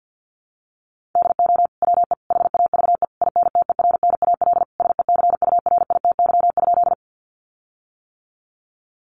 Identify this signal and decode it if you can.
Morse code, heard as a keyed tone, about 50 wpm, 700 Hz. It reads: DO WE HAVE INTERNAL SECURITY?
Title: Sentences from Top 1400 Words 50wpm